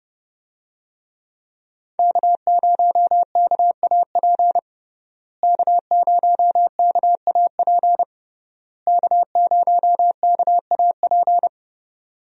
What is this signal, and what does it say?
Morse code, heard as a keyed tone, about 28 wpm, 700 Hz. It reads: K0KAP K0KAP K0KAP
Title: USA Call Signs - 5 Characters - Repeated 3x - 30wpm